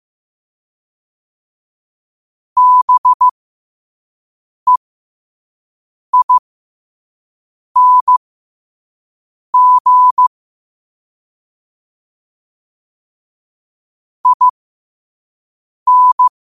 Morse code, heard as a keyed tone, about 15 wpm, spaced as for 5 wpm, 1000 Hz. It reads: BEING IN
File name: qotdfiles / 5wpm